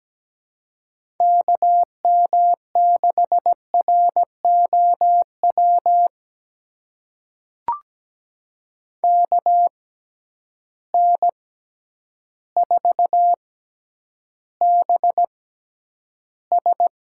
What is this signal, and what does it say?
Morse code, heard as a keyed tone, about 17 wpm, spaced as for 9 wpm, 700 Hz. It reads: KM6ROW KN4BS
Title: USA Call Signs - 6 Characters - 6x Character Spacing - 17wpm